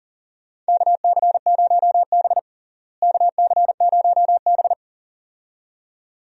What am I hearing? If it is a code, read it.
Morse code, heard as a keyed tone, about 40 wpm, 700 Hz. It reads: KC0B KC0B
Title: USA Call Signs - 4 Characters - Repeated 2x - 40wpm